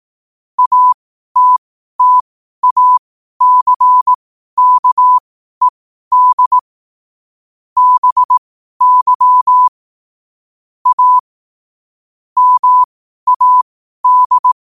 Morse code, heard as a keyed tone, about 18 wpm, spaced as for 12 wpm, 1000 Hz. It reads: ATTACKED BY A MAD
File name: qotdfiles / 12wpm